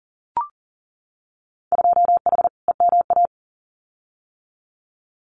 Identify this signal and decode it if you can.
Morse code, heard as a keyed tone, about 40 wpm, 700 Hz. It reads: 2H EMA
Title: Winter Field Day Exchanges 40wpm